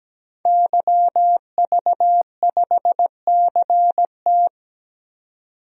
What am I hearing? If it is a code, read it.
Morse code, heard as a keyed tone, about 17 wpm, 700 Hz. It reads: YV5CT
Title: DXCC-100 Call Signs - 5 Characters - 17wpm